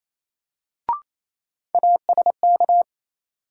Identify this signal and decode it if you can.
Morse code, heard as a keyed tone, about 28 wpm, 700 Hz. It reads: ASK
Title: Sentences from Top 200 Words - New Word Drill - 28wpm